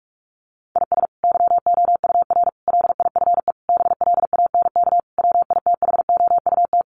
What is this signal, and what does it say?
Morse code, heard as a keyed tone, about 44 wpm, 700 Hz. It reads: IS YOUR LIFE BLANK WITHOUT
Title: Sentences from Top 1500 Words 45wpm